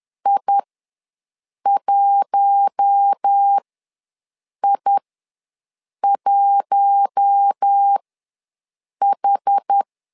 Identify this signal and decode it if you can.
Morse code, heard as a keyed tone, about 10 wpm, spaced as for 6 wpm, 800 Hz. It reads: I1I1H
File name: LEZIONI / 4 Q1 ws 10wpm 2538